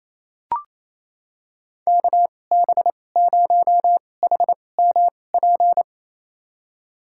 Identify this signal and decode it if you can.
Morse code, heard as a keyed tone, about 28 wpm, spaced as for 20 wpm, 700 Hz. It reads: KB0HMP
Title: USA Call Signs - 6 Characters - 2x Character Spacing - 28wpm